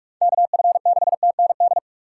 Morse code, heard as a keyed tone, about 42 wpm, 700 Hz. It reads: KF6TND
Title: USA Call Signs - 6 Characters - 3x Character Spacing - 45wpm